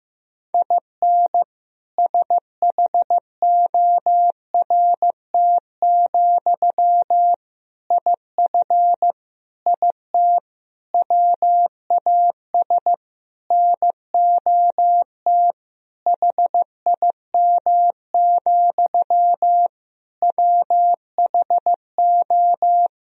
Morse code, heard as a keyed tone, about 15 wpm, 700 Hz. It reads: IN SHORT, IF IT WAS NOT HIM, WHO
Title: Sentences from Top 1000 Words - Review Part 2 - 15wpm